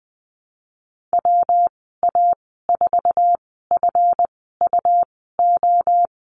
Morse code, heard as a keyed tone, about 20 wpm, spaced as for 15 wpm, 700 Hz. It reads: WA4FUO
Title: 161 Lesson 26g - Letter g - Call Signs 20wpm